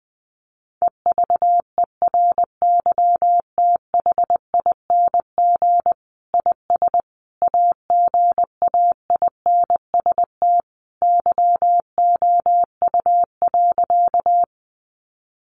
Morse code, heard as a keyed tone, about 20 wpm, 700 Hz. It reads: EVERYTHING IS AGAINST YOU.
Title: Sentences from Top 700 Words 20wpm